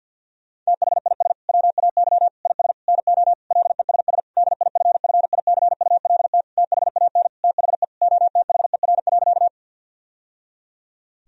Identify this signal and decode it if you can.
Morse code, heard as a keyed tone, about 52 wpm, 700 Hz. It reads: THIS WAY IS NO LESS DIFFICULT THAN THE OTHER.
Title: Sentences from Top 1000 Words - Review Part 2 - 50wpm